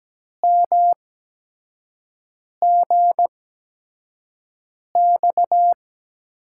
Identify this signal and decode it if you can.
Morse code, heard as a keyed tone, about 17 wpm, spaced as for 5 wpm, 700 Hz. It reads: MGX
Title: USA Call Signs - 6 Characters - 8x Character Spacing - 17wpm